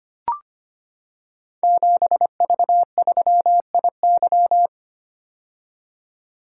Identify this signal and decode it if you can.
Morse code, heard as a keyed tone, about 25 wpm, 690 Hz. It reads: 7V3IY
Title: Code Groups - 5 Letter:Numbers - 25wpm